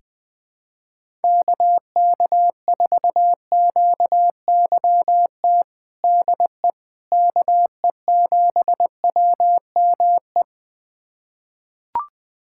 Morse code, heard as a keyed tone, about 20 wpm, 700 Hz. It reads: KK4QYT DE KE7WME
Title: USA Callsign DE Callsign - 6 Characters - 20wpm